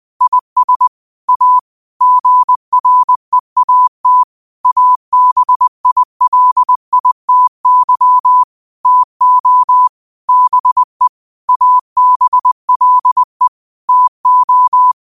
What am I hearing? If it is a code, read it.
Morse code, heard as a keyed tone, about 20 wpm, 1000 Hz. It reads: IS A GREAT ABILITY TO BE ABLE TO